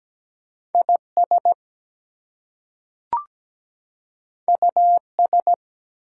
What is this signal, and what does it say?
Morse code, heard as a keyed tone, about 17 wpm, 700 Hz. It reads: IS US
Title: Two Letter Words - 17wpm